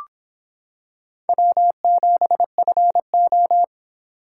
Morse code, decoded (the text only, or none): W7FO